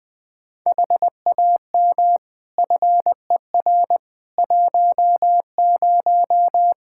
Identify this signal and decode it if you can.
Morse code, heard as a keyed tone, about 20 wpm, 700 Hz. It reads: HAM FER 10
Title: K1USN SST QSO Practice 20wpm